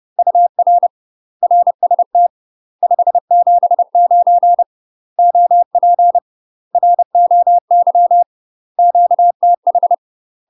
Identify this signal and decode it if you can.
Morse code, heard as a keyed tone, about 30 wpm, 700 Hz. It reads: UR RST 579 OP ROY QTH
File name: UK MASTER QSOs TRAINING / 30WPM / UK MASTER QSOs 0026